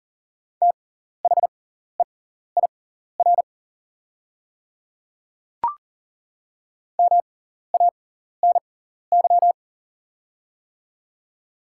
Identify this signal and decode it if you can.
Morse code, heard as a keyed tone, about 40 wpm, spaced as for 14 wpm, 700 Hz. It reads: THEIR MANY
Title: 6x Character Spacing - Top 100 Words - 40wpm